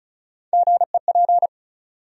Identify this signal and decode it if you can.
Morse code, heard as a keyed tone, about 34 wpm, 700 Hz. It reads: GEP
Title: USA Call Sign Postfixes - 3 Letters 35wpm